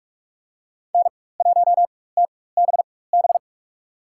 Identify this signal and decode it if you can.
Morse code, heard as a keyed tone, about 43 wpm, spaced as for 21 wpm, 700 Hz. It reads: N1TBB
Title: USA Call Signs - 4x Character Spacing - 45wpm